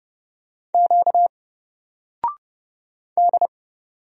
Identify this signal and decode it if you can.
Morse code, heard as a keyed tone, about 30 wpm, 700 Hz. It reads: Q D
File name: Single-Letter-Number-30wpm